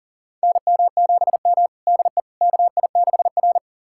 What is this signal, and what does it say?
Morse code, heard as a keyed tone, about 40 wpm, 700 Hz. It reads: NM7M DE KI6R
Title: USA Callsign DE Callsign - 4 Characters - 40wpm